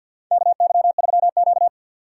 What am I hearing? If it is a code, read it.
Morse code, heard as a keyed tone, about 52 wpm, 690 Hz. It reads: KY3Q
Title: USA Call Signs - 2x Character Spacing - 50wpm